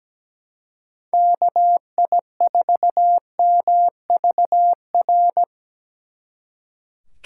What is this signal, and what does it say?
Morse code, heard as a keyed tone, about 20 wpm, 700 Hz. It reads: KI4MVR
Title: Speed Racing - USA 2x3 Call Signs 17wpm